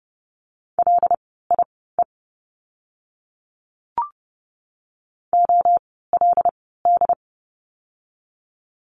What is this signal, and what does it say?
Morse code, heard as a keyed tone, about 30 wpm, spaced as for 17 wpm, 700 Hz. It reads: LIE OLD
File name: Lesson 16f - Letter c - Words 30wpm